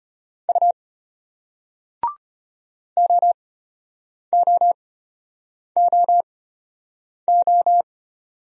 Morse code, heard as a keyed tone, about 35 wpm, spaced as for 8 wpm, 700 Hz. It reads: U OOOO